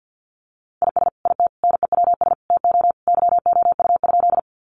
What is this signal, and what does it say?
Morse code, heard as a keyed tone, about 50 wpm, 700 Hz. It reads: IS IT NEWS TO YOU?
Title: Sentences from Top 800 Words 50wpm